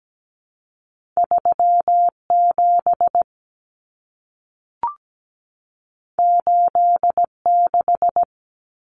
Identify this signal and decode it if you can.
Morse code, heard as a keyed tone, about 17 wpm, 700 Hz. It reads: 37 86
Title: Common Cut Numbers - 2-Digit 17wpm